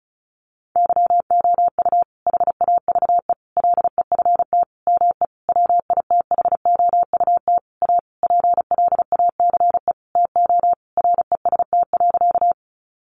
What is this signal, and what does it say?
Morse code, heard as a keyed tone, about 34 wpm, 700 Hz. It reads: YOU HAVE LEFT ME WITHOUT A PLACE TO REST.
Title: Sentences from Top 600 Words 35wpm